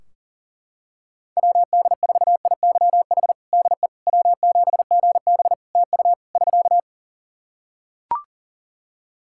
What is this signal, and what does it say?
Morse code, heard as a keyed tone, about 37 wpm, 700 Hz. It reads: WD4IYH DE W7GB TU <SK>
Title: Practice QSO - Exchange 4 - Ending - 40wpm